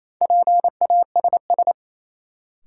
Morse code, heard as a keyed tone, about 28 wpm, 700 Hz. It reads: PASS
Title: Sentences from Top 400 Words - New Word Drill - 28wpm